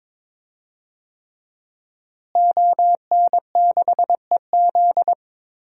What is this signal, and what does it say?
Morse code, heard as a keyed tone, about 21 wpm, 700 Hz. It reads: ON6EZ
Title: DXCC-100 Call Signs - 5 Characters - Repeated 3x - 22wpm